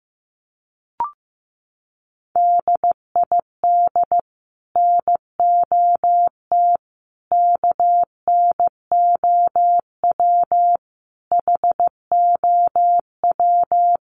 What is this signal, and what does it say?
Morse code, heard as a keyed tone, about 15 wpm, 700 Hz. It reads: DID NOT KNOW HOW
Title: Sets of 5 Words 15wpm